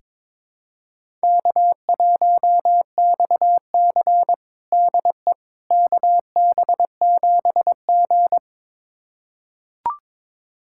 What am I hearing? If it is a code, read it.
Morse code, heard as a keyed tone, about 21 wpm, 700 Hz. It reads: K1XC DE KB7G
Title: USA Callsign DE Callsign - 4 Characters - 22wpm